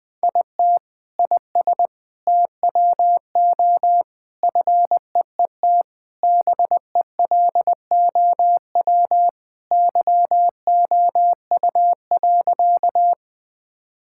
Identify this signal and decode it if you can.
Morse code, heard as a keyed tone, about 20 wpm, 700 Hz. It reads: IT IS TWO FEET BELOW YOU.
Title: Sentences from Top 1000 Words - Review Part 5 - 20wpm